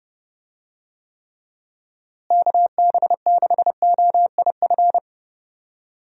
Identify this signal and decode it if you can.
Morse code, heard as a keyed tone, about 32 wpm, 700 Hz. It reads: KB6OIF
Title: USA Call Signs - 6 Characters - Repeated 2x - 30wpm